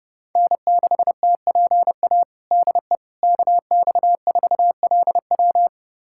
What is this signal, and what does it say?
Morse code, heard as a keyed tone, about 30 wpm, 700 Hz. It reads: N6TPA DE KX4LW
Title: USA Callsign DE Callsign - 5 Characters - 30wpm